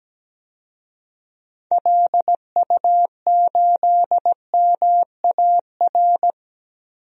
Morse code, heard as a keyed tone, about 17 wpm, 700 Hz. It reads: LU8MAR
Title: DXCC-100 Call Signs - 6 Characters - Repeated 2x - 17wpm